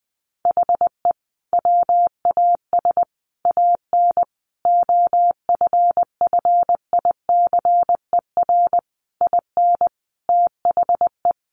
Morse code, heard as a keyed tone, about 20 wpm, 700 Hz. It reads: HE WAS AN OFFICER IN THE
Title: Sentences from Top 1200 Words 20wpm